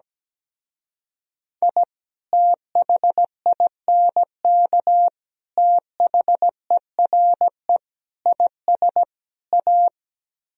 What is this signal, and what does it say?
Morse code, heard as a keyed tone, about 17 wpm, 700 Hz. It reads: I THINK THERE IS A
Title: Speed-Racing - Sets of 5 Words 17wpm